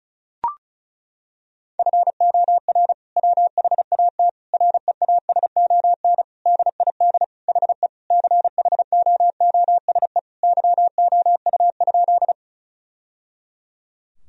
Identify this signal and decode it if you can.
Morse code, heard as a keyed tone, about 34 wpm, 700 Hz. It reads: FOR WHAT REASON DID HE CHOOSE YOU?
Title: Sentences from Top 1000 Words - Review Part 4 - 35wpm